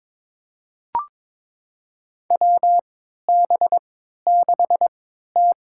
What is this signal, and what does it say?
Morse code, heard as a keyed tone, about 22 wpm, spaced as for 12 wpm, 700 Hz. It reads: WB6T
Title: USA Call Signs - 4 Characters - 3x Character Spacing - 22wpm